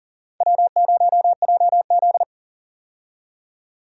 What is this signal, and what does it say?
Morse code, heard as a keyed tone, about 40 wpm, 690 Hz. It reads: W0JZ
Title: USA Call Signs - 4 Characters - 40wpm